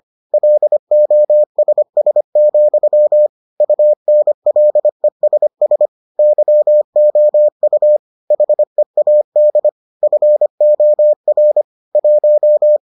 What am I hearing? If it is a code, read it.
Morse code, heard as a keyed tone, about 25 wpm, 600 Hz. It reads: LOSS, UNLESS YOU HEAD FOR 1